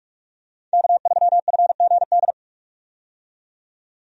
Morse code, heard as a keyed tone, about 46 wpm, 700 Hz. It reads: K3FGD
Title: USA Call Signs - 5 Characters - 45wpm